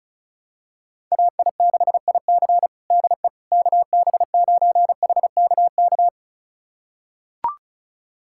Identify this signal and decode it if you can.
Morse code, heard as a keyed tone, about 36 wpm, 700 Hz. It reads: AI6IC DE KB9HKK
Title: USA Callsign DE Callsign 35wpm